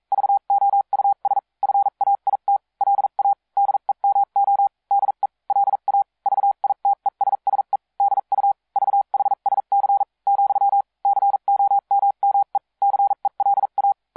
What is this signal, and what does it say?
Morse code, heard as a keyed tone, about 42 wpm, spaced as for 36 wpm, 800 Hz. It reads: VOUS FAIT LA DEMO DE LA VITESSE DU VHSC , COMME CELA